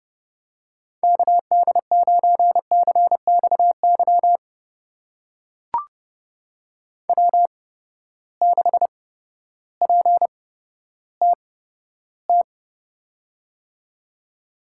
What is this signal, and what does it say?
Morse code, heard as a keyed tone, about 30 wpm, spaced as for 14 wpm, 700 Hz. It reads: KD9CXY W6PTT
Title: USA Call Signs - 8x Character Spacing - 30wpm